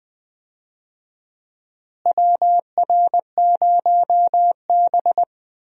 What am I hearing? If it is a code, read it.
Morse code, heard as a keyed tone, about 20 wpm, 700 Hz. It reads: WR0B
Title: USA Call Signs - 4 Characters - Repeated 2x - 20wpm